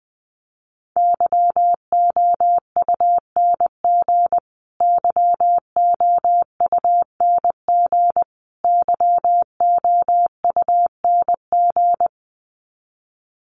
Morse code, heard as a keyed tone, about 20 wpm, 700 Hz. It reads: YOUNG YOUNG YOUNG
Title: Top 500 Words Repeated 20wpm